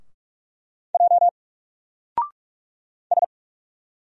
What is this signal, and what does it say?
Morse code, heard as a keyed tone, about 44 wpm, 700 Hz. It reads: J S
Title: Single Letter-Number 45wpm